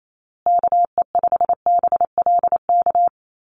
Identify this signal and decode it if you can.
Morse code, heard as a keyed tone, about 28 wpm, 700 Hz. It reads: KE5BLK
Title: USA Call Signs Repeated 2x 28wpm